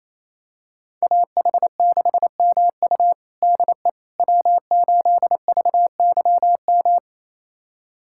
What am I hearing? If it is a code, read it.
Morse code, heard as a keyed tone, about 28 wpm, 700 Hz. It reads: AH6MU DE W8VYM
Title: USA Callsign DE Callsign - 5 Characters - 28wpm